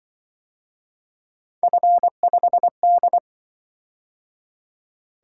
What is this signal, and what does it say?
Morse code, heard as a keyed tone, about 24 wpm, 700 Hz. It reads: F5D
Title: Speed-Racing - Three Letter-Number Combinations - 20wpm